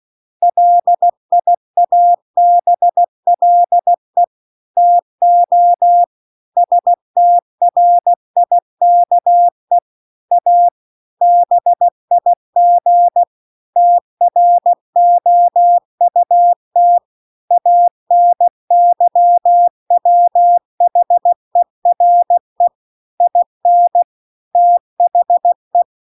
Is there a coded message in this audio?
Yes: Morse code from 0.4 to 25.8 s, heard as a keyed tone, about 16 wpm, 700 Hz. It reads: LIABLE TO STRIKE A BIG TROUT ANYWHERE IN THE